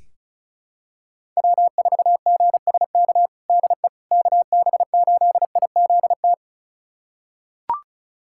Morse code, heard as a keyed tone, about 36 wpm, 700 Hz. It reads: W4GSK DE KB8IZT
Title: USA Callsign DE Callsign 35wpm